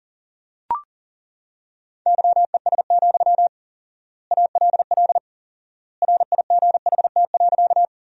Morse code, heard as a keyed tone, about 37 wpm, 700 Hz. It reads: YES, ALL RIGHT.
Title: 4x Word-Spacing - Sentences from Top 500 Words - 40wpm